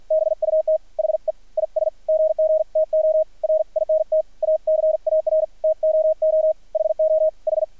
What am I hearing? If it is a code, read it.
Morse code, heard as a keyed tone, about 53 wpm, spaced as for 45 wpm, 630 Hz, with radio noise. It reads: BUT HE IS GG TO RINT A CAR TOO HOH
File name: archived / CRSnet